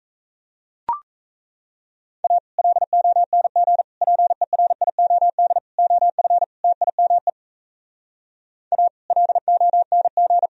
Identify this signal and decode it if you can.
Morse code, heard as a keyed tone, about 37 wpm, 700 Hz. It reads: A LONG PERIOD OF TIME A LONG